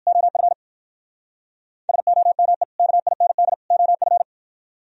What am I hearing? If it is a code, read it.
Morse code, heard as a keyed tone, about 60 wpm, 700 Hz. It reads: OF SOME KIND OF